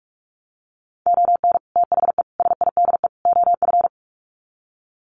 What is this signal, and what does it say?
Morse code, heard as a keyed tone, about 44 wpm, 700 Hz. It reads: ON THE SIDE OF
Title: Sets of 4 Words 45wpm